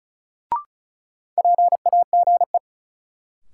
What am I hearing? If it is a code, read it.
Morse code, heard as a keyed tone, about 36 wpm, 700 Hz. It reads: PAGE
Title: Sentences from Top 1000 Words - Review Part 1 - New Word Drill - 35wpm